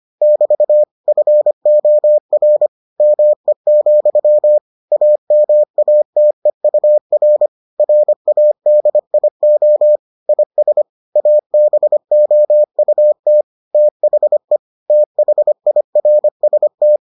Morse code, heard as a keyed tone, about 25 wpm, 600 Hz. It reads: <BT> FOR ME, AMATEUR RADIO IS ABOUT THE THIRST